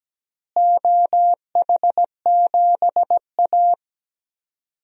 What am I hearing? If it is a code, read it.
Morse code, heard as a keyed tone, about 17 wpm, 700 Hz. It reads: OH7A
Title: DXCC-100 Call Signs - 4 Characters - Repeated 2x - 17wpm